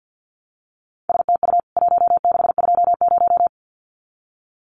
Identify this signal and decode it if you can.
Morse code, heard as a keyed tone, about 50 wpm, 700 Hz. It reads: STU 1620
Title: CWT Exchange 50wpm